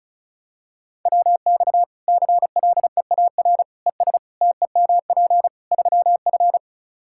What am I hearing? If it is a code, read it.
Morse code, heard as a keyed tone, about 34 wpm, 700 Hz. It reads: WX CLEAR ES TEMP 3F
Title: Practice QSO - Exchange 2 - Info - 35wpm